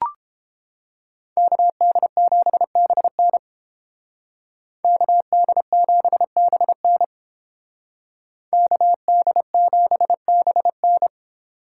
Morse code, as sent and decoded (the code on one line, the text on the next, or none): -.- -.. --... -... -. / -.- -.. --... -... -. / -.- -.. --... -... -.
KD7BN KD7BN KD7BN